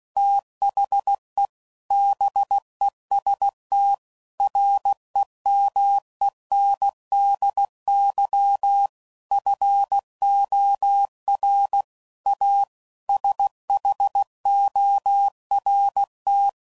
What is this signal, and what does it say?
Morse code, heard as a keyed tone, about 16 wpm, 800 Hz. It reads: THE BEST REMENDY FOR A SHORT